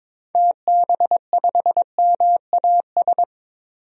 Morse code, heard as a keyed tone, about 22 wpm, 700 Hz. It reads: TB5MAS
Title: DXCC-100 Call Signs - 6 Characters - 22wpm